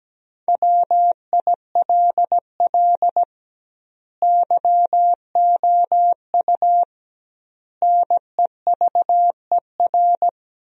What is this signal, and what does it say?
Morse code, heard as a keyed tone, about 17 wpm, 700 Hz. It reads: WILL YOU NEVER
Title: 2x Word-Spacing - Sentences from Top 300 Words - 17wpm